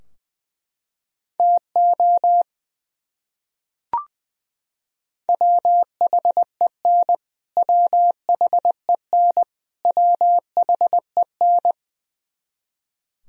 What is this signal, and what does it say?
Morse code, heard as a keyed tone, about 20 wpm, 700 Hz. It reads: TO WHEN WHEN WHEN
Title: Top 100 Words Repeated 20wpm